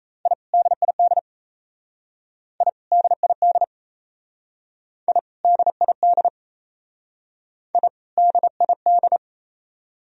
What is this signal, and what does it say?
Morse code, heard as a keyed tone, about 35 wpm, 700 Hz. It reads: I DID I DID I DID I DID